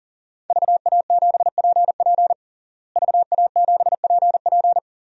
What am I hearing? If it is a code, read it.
Morse code, heard as a keyed tone, about 40 wpm, 700 Hz. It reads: VA7PP VA7PP
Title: DXCC-100 Call Signs - 5 Characters - Repeated 3x - 40wpm